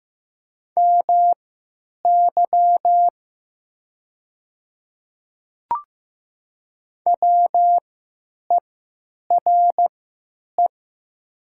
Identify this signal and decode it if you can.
Morse code, heard as a keyed tone, about 15 wpm, spaced as for 9 wpm, 700 Hz. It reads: MY WERE
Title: 3x Character Spacing - Top 100 Words - 15wpm